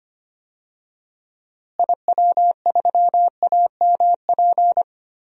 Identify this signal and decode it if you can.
Morse code, heard as a keyed tone, about 25 wpm, 700 Hz. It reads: IW3AMP
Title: DXCC-100 Call Signs - 6 Characters - Repeated 3x - 25wpm